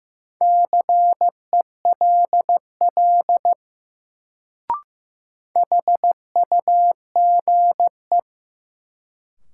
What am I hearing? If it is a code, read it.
Morse code, heard as a keyed tone, about 15 wpm, 700 Hz. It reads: CELL HUGE
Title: Sentences from Top 1100 Words - New Word Drill - 15wpm